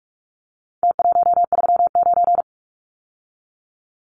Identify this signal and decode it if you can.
Morse code, heard as a keyed tone, about 44 wpm, 700 Hz. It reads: T139
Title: Numbers - Contest Style - 4-Digit 45wpm